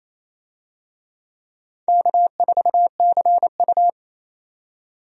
Morse code, heard as a keyed tone, about 28 wpm, 700 Hz. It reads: K4CU
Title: USA Call Signs - 4 Characters - 28wpm